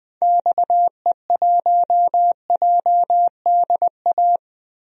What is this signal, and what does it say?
Morse code, heard as a keyed tone, about 21 wpm, 700 Hz. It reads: XE1JDA
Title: DXCC-100 Call Signs - 6 Characters - 20wpm